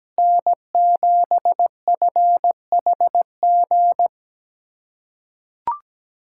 Morse code, heard as a keyed tone, about 17 wpm, 700 Hz. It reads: N7FHG
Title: USA Callsign DE Callsign - 5 Characters - 17wpm